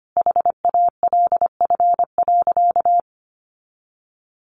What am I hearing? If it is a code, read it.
Morse code, heard as a keyed tone, about 25 wpm, 700 Hz. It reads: HALF.
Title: Sentences from Top 400 Words 25wpm